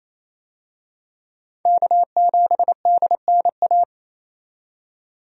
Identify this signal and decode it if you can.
Morse code, heard as a keyed tone, about 28 wpm, 700 Hz. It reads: K7DNA
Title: USA Call Signs - 5 Characters - 28wpm